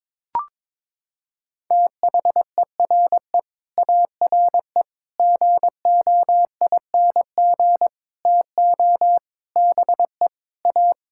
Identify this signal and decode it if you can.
Morse code, heard as a keyed tone, about 22 wpm, 700 Hz. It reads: THERE ARE GOING TO BE A
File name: practice / 22wpm / Sets of 8 Words - 22wpm